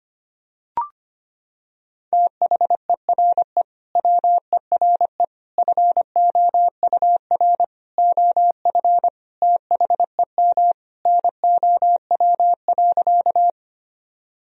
Morse code, heard as a keyed tone, about 25 wpm, 700 Hz. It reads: THERE WERE FOUR OF THEM NOW.
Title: Sentences from Top 1000 Words - Review Part 1 - 25wpm